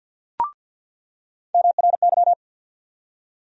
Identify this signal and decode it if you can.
Morse code, heard as a keyed tone, about 49 wpm, 700 Hz. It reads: MRY
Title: USA Call Sign Postfixes - 3 Letters 50wpm